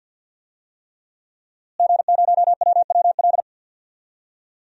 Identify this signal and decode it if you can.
Morse code, heard as a keyed tone, about 50 wpm, 700 Hz. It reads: G0WWL
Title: DXCC-100 Call Signs - 5 Characters - Repeated 3x - 50wpm